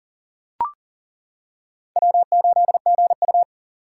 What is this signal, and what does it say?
Morse code, heard as a keyed tone, about 40 wpm, 700 Hz. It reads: W8GU
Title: K1USN SST QSO Practice - 40wpm